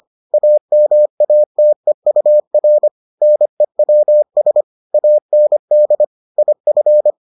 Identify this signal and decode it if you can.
Morse code, heard as a keyed tone, about 25 wpm, 600 Hz. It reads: AMATEUR NEWS AND IF